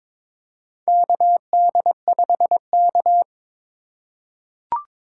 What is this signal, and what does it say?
Morse code, heard as a keyed tone, about 22 wpm, 700 Hz. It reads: KD5K
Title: USA Call Signs - 4 Characters - 2x Character Spacing - 22wpm